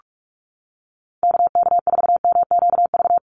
Morse code, heard as a keyed tone, about 46 wpm, 700 Hz. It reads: KK4MQV
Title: USA Call Signs 45wpm